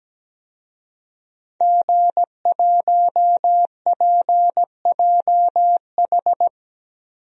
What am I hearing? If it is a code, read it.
Morse code, heard as a keyed tone, about 17 wpm, 700 Hz. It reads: G1PJH
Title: DXCC-100 Call Signs - 5 Characters - Repeated 3x - 17wpm